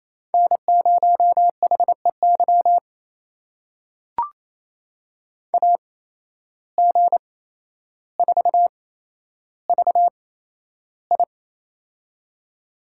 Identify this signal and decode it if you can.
Morse code, heard as a keyed tone, about 28 wpm, spaced as for 12 wpm, 700 Hz. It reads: N0HEY AG4VI
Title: USA Call Signs - 5 Characters - 8x Character Spacing - 28wpm